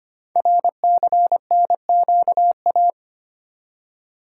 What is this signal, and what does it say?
Morse code, heard as a keyed tone, about 25 wpm, 700 Hz. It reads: RCNQA
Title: Code Groups - 5 Letters - 25wpm